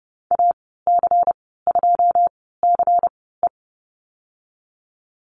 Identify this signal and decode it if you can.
Morse code, heard as a keyed tone, about 30 wpm, spaced as for 17 wpm, 700 Hz. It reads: AC2CE
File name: Lesson 19g - Letter w - Call Signs 30wpm